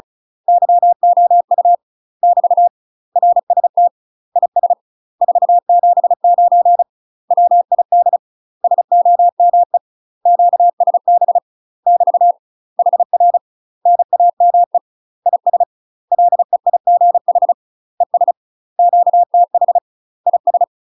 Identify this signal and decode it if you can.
Morse code, heard as a keyed tone, about 36 wpm, 700 Hz. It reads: YOU <BT> RST IS 479 WID SOME QSB <BT> HR NAME IS LEIGH ES QTH IS
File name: UK MASTER QSOs TRAINING / 35WPM / UK MASTER QSOs 0031